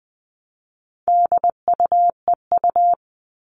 Morse code, heard as a keyed tone, about 20 wpm, 700 Hz. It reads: DUEU
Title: All Cut Numbers - 4-Digit 20wpm